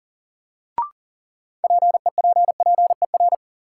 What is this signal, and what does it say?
Morse code, heard as a keyed tone, about 40 wpm, 700 Hz. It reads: PEPPER
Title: Sets of 5 Words - New Word Drill - 40wpm